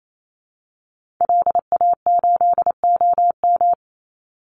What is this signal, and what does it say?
Morse code, heard as a keyed tone, about 28 wpm, 700 Hz. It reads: LA8OM
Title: Call Signs Repeated 3x - 28wpm